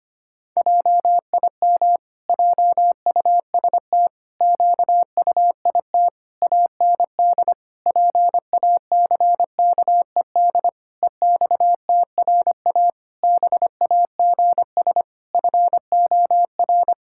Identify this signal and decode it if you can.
Morse code, heard as a keyed tone, about 25 wpm, 700 Hz. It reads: JIM JUST QUIT AND PACKED EXTRA BAGS FOR
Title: Pangram Challenge v2 - 25wpm